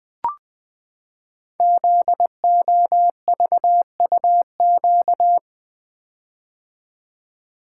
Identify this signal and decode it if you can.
Morse code, heard as a keyed tone, about 20 wpm, 700 Hz. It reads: ZOVUQ
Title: Code Groups - 5 Letters - 20wpm